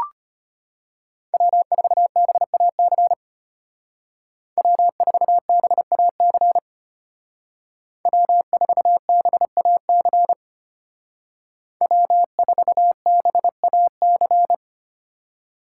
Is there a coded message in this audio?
Yes: Morse code from 1.3 to 14.5 s, heard as a keyed tone, about 33 wpm, 700 Hz. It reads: W4BAC W4BAC W4BAC W4BAC